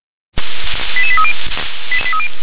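Five-tone selective call (ZVEI), heard as quick runs of five tones, tones of about 70 ms. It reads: 88830
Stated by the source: ZVEI five-tone selective call received on an SDR.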